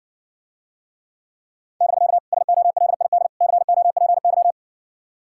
Morse code, heard as a keyed tone, about 60 wpm, 700 Hz. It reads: <BK> SOLID COPY